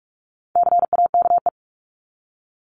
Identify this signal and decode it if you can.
Morse code, heard as a keyed tone, about 44 wpm, 700 Hz. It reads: CAKE
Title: Top 1501-1750 Words 45wpm